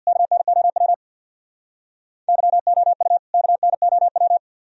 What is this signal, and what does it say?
Morse code, heard as a keyed tone, about 65 wpm, 700 Hz. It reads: KNOW YOU KNOW